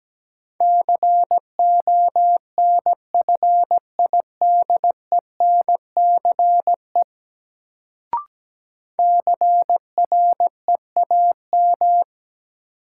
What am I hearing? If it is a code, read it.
Morse code, heard as a keyed tone, about 17 wpm, 700 Hz. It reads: CONFIDENCE CREAM
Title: Sentences from Top 1500 Words - New Word Drill - 17wpm